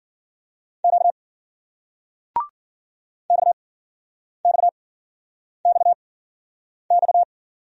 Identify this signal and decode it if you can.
Morse code, heard as a keyed tone, about 56 wpm, spaced as for 9 wpm, 700 Hz. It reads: Q XXXX